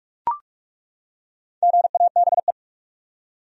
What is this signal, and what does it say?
Morse code, heard as a keyed tone, about 43 wpm, 700 Hz. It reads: GABE
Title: USA Common Names - 45wpm